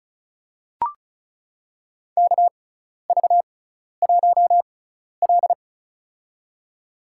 Morse code, heard as a keyed tone, about 34 wpm, spaced as for 12 wpm, 700 Hz. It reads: KV1L
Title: USA Call Signs - 4 Characters - 6x Character Spacing - 35wpm